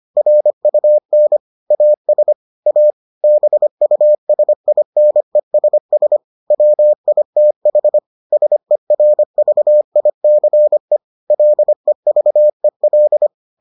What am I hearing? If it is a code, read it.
Morse code, heard as a keyed tone, about 25 wpm, 600 Hz. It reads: RUN AS A BUSINESS WITH SERVICE LEVEL